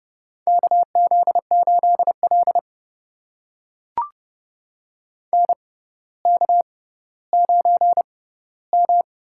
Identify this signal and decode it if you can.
Morse code, heard as a keyed tone, about 32 wpm, spaced as for 16 wpm, 700 Hz. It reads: KZ8L NK9M
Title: USA Call Signs - 4 Characters - 6x Character Spacing - 30wpm